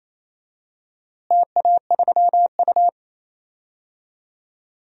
Morse code, heard as a keyed tone, about 28 wpm, 700 Hz. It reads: TA3U
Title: DXCC-100 Call Signs - 4 Characters - Repeated 2x - 28wpm